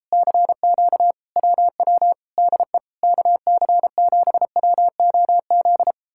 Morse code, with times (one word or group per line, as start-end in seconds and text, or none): 0.1-1.1 CQ
1.4-2.1 WW
2.4-2.8 DE
3.0-5.9 KC7WOZ